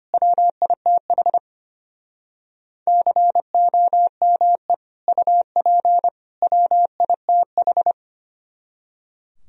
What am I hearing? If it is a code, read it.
Morse code, heard as a keyed tone, about 33 wpm, 700 Hz. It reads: WITH COME UP WITH